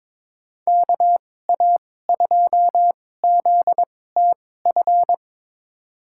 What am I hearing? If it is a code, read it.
Morse code, heard as a keyed tone, about 22 wpm, spaced as for 16 wpm, 700 Hz. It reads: KA2ZTF
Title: USA Call Signs - 6 Characters - 2x Character Spacing - 22wpm